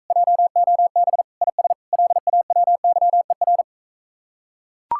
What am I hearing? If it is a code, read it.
Morse code, heard as a keyed tone, about 44 wpm, 700 Hz. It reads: JOB IS LAWYER